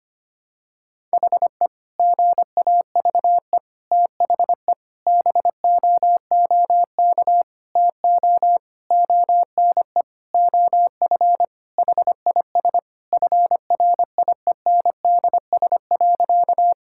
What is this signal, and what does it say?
Morse code, heard as a keyed tone, about 25 wpm, 700 Hz. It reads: HE GAVE THE BOOK TO ONE OF HIS FRIENDS.
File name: practice / 25wpm / Sentences with 4-Word Phrases - 25wpm